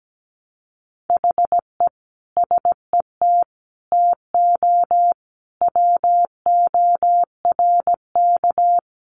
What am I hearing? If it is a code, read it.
Morse code, heard as a keyed tone, about 17 wpm, 700 Hz. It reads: HE SET TO WORK
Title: Sentences from Top 1200 Words 17wpm